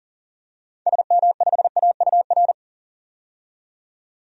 Morse code, heard as a keyed tone, about 40 wpm, 700 Hz. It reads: SM5AUR
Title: DXCC-100 Call Signs - 6 Characters - 40wpm